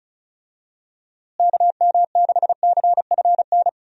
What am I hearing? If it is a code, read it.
Morse code, heard as a keyed tone, about 36 wpm, 700 Hz. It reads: KM6CFN
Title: USA Call Signs - 6 Characters - Repeated 2x - 35wpm